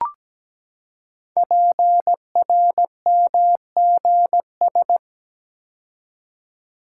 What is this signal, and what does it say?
Morse code, heard as a keyed tone, about 17 wpm, 700 Hz. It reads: PRMGS
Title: Code Groups - 5 Letters - 17wpm